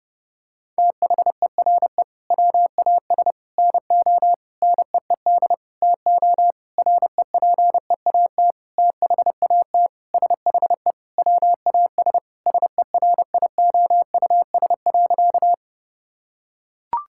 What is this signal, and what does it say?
Morse code, heard as a keyed tone, about 30 wpm, 700 Hz. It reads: THERE WAS NO NEED TO REPEAT THAT SHE WAS SERIOUS.
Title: Sentences from Top 1600 Words 30wpm